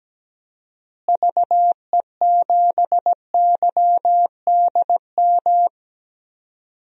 Morse code, heard as a keyed tone, about 17 wpm, 700 Hz. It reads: VE7YDM
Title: DXCC-100 Call Signs - 6 Characters - Repeated 2x - 17wpm